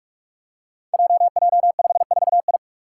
Morse code, heard as a keyed tone, about 46 wpm, 700 Hz. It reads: JJ54I
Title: Five Challenging Letter-Number Combinations 45wpm